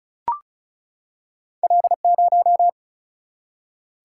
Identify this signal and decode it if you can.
Morse code, heard as a keyed tone, about 36 wpm, 700 Hz. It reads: L0
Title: Single-Letter Single-Number 35wpm